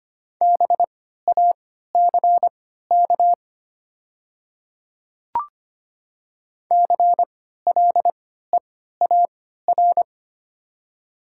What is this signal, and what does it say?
Morse code, heard as a keyed tone, about 25 wpm, spaced as for 14 wpm, 700 Hz. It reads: BACK CLEAR
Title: 3x Character Spacing - Top 500 Words - 25wpm